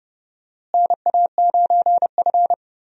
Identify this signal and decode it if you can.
Morse code, heard as a keyed tone, about 30 wpm, 700 Hz. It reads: NA9F
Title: USA Call Signs - 4 Characters - Repeated 2x - 30wpm